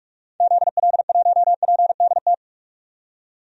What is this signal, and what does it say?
Morse code, heard as a keyed tone, about 45 wpm, 700 Hz. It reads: ZL1PDT
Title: DXCC-100 Call Signs - 6 Characters - 45wpm